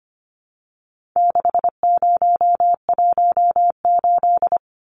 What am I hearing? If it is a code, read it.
Morse code, heard as a keyed tone, about 25 wpm, 700 Hz. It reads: 6018
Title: Numbers - 4-Digit 25wpm